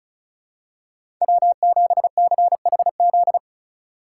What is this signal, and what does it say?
Morse code, heard as a keyed tone, about 36 wpm, 700 Hz. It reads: W7CHZ
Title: USA Call Signs - 4x Character Spacing - 35wpm